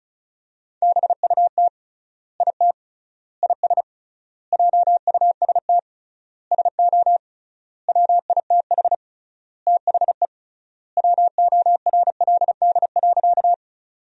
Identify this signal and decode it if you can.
Morse code, heard as a keyed tone, about 34 wpm, 700 Hz. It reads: BUT IT IS JUST SO WITH THE WORLD.
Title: 3x Word-Spacing - Sentences from Top 200 Words - 35wpm